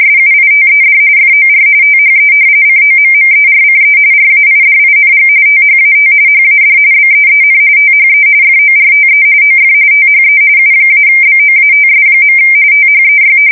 RTTY, about 45.45 baud. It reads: SUITED FOR THE COMMERCE RAIDING THAT HAD BEEN THE FORTE OF THE U-BOATS. THEY
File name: fsk170 / usnavy4